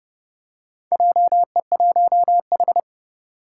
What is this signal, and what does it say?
Morse code, heard as a keyed tone, about 32 wpm, 700 Hz. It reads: JE1H
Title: Four Challenging Letter-Number Combinations 30wpm